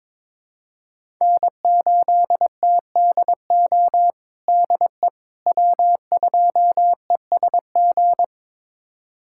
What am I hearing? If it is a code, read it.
Morse code, heard as a keyed tone, about 22 wpm, 700 Hz. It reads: N8TDO DE W2ESG
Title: USA Callsign DE Callsign - 5 Characters - 22wpm